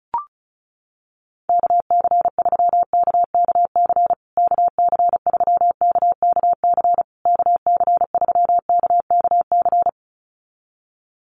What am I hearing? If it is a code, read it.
Morse code, heard as a keyed tone, about 34 wpm, 700 Hz. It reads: KC3KKC KC3KKC KC3KKC
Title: USA Call Signs Repeated 3x 35wpm